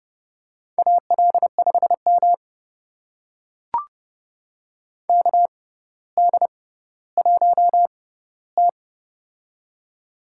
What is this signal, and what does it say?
Morse code, heard as a keyed tone, about 32 wpm, spaced as for 16 wpm, 700 Hz. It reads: AL5M KD1T
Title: USA Call Signs - 4 Characters - 6x Character Spacing - 30wpm